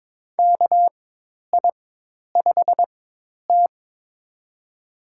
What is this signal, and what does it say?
Morse code, heard as a keyed tone, about 21 wpm, spaced as for 10 wpm, 700 Hz. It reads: KI5T
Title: USA Call Signs - 4x Character Spacing - 22wpm